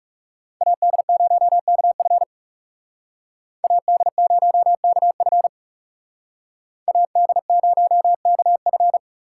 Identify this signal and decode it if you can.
Morse code, heard as a keyed tone, about 39 wpm, 700 Hz. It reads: AD0KF AD0KF AD0KF